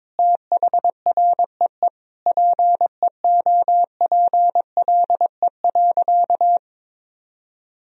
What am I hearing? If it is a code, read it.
Morse code, heard as a keyed tone, about 21 wpm, 700 Hz. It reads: THREE PEOPLE.
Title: Sentences from Top 1000 Words - Review Part 1 - 22wpm